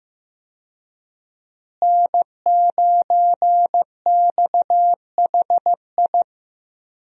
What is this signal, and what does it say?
Morse code, heard as a keyed tone, about 15 wpm, 700 Hz. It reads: N9XHI
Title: USA Call Signs - 5 Characters - Repeated 3x - 15wpm